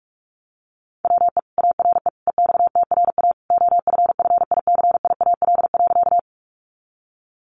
Morse code, heard as a keyed tone, about 46 wpm, 700 Hz. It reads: WE ARE EXTRA OFFICIAL.
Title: Sentences from Top 1000 Words 45wpm